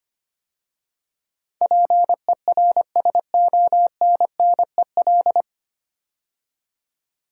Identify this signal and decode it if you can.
Morse code, heard as a keyed tone, about 25 wpm, 700 Hz. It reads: PERSONNEL
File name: 9 Letter Words - First 3 Letters Given - 25wpm